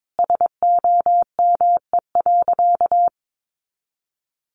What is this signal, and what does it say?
Morse code, heard as a keyed tone, about 22 wpm, 700 Hz. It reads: SOME.
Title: Sentences from Top 100 Words 22wpm